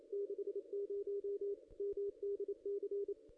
Morse code, heard as a keyed tone, about 28 wpm, 410 Hz. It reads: 60 MDC
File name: contests / ss-cw-k5tr-op-n5rz -0400